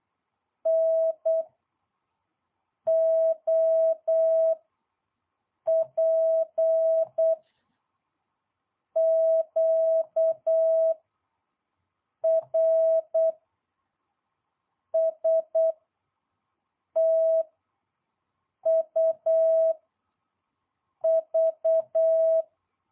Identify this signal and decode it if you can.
Morse code, heard as a keyed tone, about 8 wpm, 650 Hz. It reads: N O P Q R S T U V